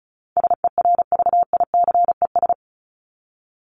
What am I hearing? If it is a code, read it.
Morse code, heard as a keyed tone, about 34 wpm, 700 Hz. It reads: SERVICES
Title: ES:S Suffix - Top 100 Words with Base Words 35wpm